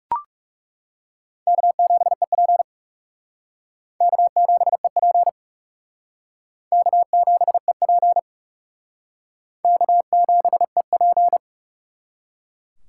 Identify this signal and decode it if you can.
Morse code, heard as a keyed tone, about 35 wpm, 700 Hz. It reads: K7EP K7EP K7EP K7EP
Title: Speed Racing - USA Call Signs 30wpm